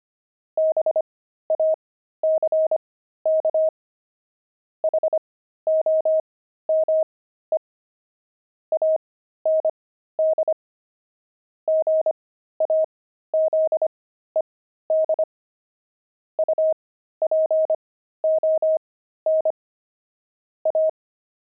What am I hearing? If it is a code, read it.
Morse code, heard as a keyed tone, about 25 wpm, spaced as for 13 wpm, 630 Hz. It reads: BACK HOME AND GAZED UPON A
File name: pangram-fs-13-cs-25-jay